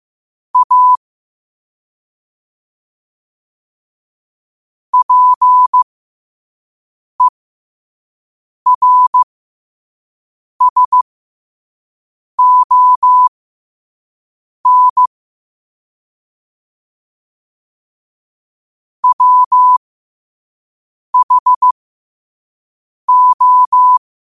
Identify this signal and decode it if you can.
Morse code, heard as a keyed tone, about 15 wpm, spaced as for 5 wpm, 1000 Hz. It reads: A PERSON WHO
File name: qotdfiles / 5wpm